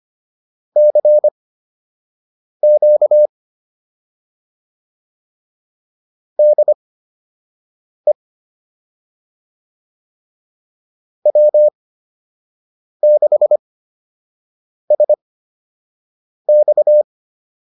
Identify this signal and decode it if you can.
Morse code, heard as a keyed tone, about 25 wpm, spaced as for 6 wpm, 600 Hz. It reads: CQ DE W6SX
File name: QSO 102 06